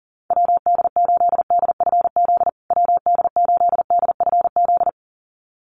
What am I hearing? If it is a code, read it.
Morse code, heard as a keyed tone, about 37 wpm, 700 Hz. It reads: WD8DFZ WD8DFZ
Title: USA Call Signs Repeated 2x 40wpm